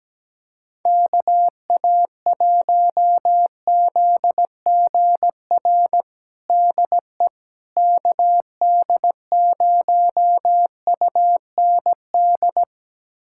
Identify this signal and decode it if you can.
Morse code, heard as a keyed tone, about 17 wpm, 700 Hz. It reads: KA1ZGR DE KD0UND
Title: USA Callsign DE Callsign - 6 Characters - 17wpm